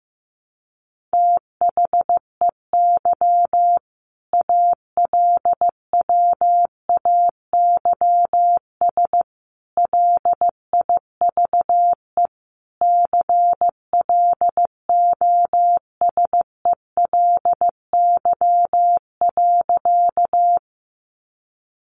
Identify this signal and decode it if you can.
Morse code, heard as a keyed tone, about 15 wpm, 700 Hz. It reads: THEY ALWAYS LIVE CLOSELY.
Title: Sentences from Top 1400 Words 15wpm